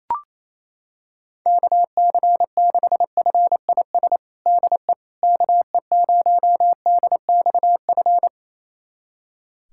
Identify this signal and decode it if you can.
Morse code, heard as a keyed tone, about 28 wpm, 700 Hz. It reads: KC6FIS DE KE0DXF
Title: USA Callsign DE Callsign - 6 Characters - 28wpm